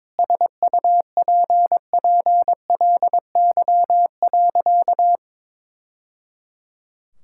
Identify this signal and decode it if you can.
Morse code, heard as a keyed tone, about 22 wpm, 700 Hz. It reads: SUPPLY.
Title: Sentences from Top 1000 Words - Review Part 5 - 22wpm